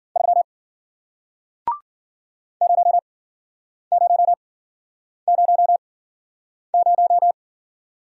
Morse code, heard as a keyed tone, about 52 wpm, spaced as for 9 wpm, 700 Hz. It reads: 3 0000